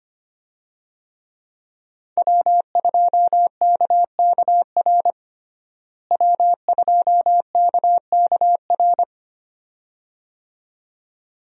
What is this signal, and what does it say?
Morse code, heard as a keyed tone, about 25 wpm, 700 Hz. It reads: W2KKR W2KKR
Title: USA Call Signs - 5 Characters - Repeated 2x - 25wpm